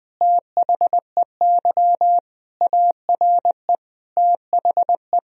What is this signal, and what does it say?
Morse code, heard as a keyed tone, about 20 wpm, 700 Hz. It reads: THEY ARE THE
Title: Sentences from Top 1000 Words - Review Part 4 - 20wpm